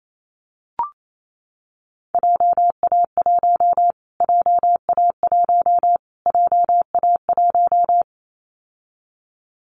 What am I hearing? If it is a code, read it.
Morse code, heard as a keyed tone, about 28 wpm, 700 Hz. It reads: JA1 JA1 JA1
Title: Callsign Prefixes for Top 25 DXCC - Repeated 3x - 28wpm